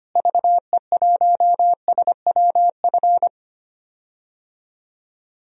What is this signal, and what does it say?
Morse code, heard as a keyed tone, about 25 wpm, 700 Hz. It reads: VE1SWF
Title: DXCC-100 Call Signs - 6 Characters - Repeated 2x - 25wpm